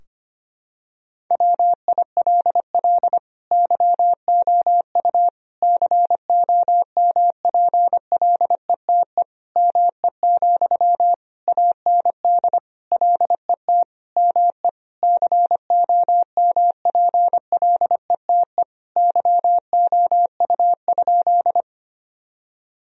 Morse code, heard as a keyed tone, about 25 wpm, 700 Hz. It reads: WILL YOU COMPLETE ME, AND LET ME COMPLETE YOU?
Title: Sentences from Top 1000 Words - Review Part 2 - 25wpm